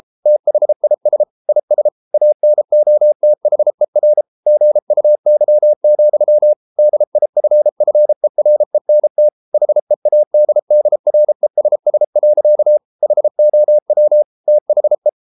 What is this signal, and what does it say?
Morse code, heard as a keyed tone, about 33 wpm, 600 Hz. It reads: THIS IS ANOTHER GUY, DIFFERENT HEADDRESS. HOW THE